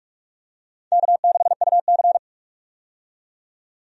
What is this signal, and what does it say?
Morse code, heard as a keyed tone, about 46 wpm, 700 Hz. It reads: K6UC
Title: USA Call Signs - 4 Characters - 45wpm